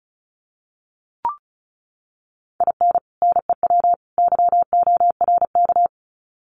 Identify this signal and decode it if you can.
Morse code, heard as a keyed tone, about 34 wpm, 700 Hz. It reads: IN NEW YORK
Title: Sets of 3 Words 35wpm